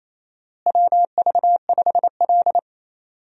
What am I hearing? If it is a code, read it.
Morse code, heard as a keyed tone, about 28 wpm, 700 Hz. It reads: WV5L
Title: USA Call Signs - 4 Characters - Repeated 2x - 28wpm